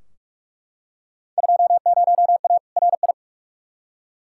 Morse code, heard as a keyed tone, about 43 wpm, 700 Hz. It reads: 20A RI
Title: Field Day Exchanges Encore - 45wpm